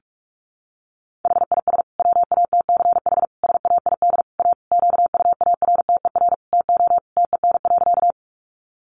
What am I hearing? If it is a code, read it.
Morse code, heard as a keyed tone, about 44 wpm, 700 Hz. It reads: HIS WATCH SAID A QUARTER TO TEN.
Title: Sentences from Top 1300 Words 45wpm